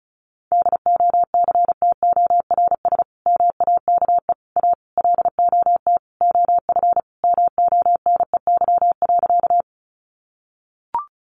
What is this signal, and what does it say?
Morse code, heard as a keyed tone, about 34 wpm, 700 Hz. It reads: DOCTORS MAKE A LOT OF MONEY.
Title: Easy Phrases 35wpm